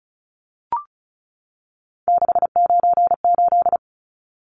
Morse code, heard as a keyed tone, about 34 wpm, 700 Hz. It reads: 698
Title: Numbers - 3-Digit 35wpm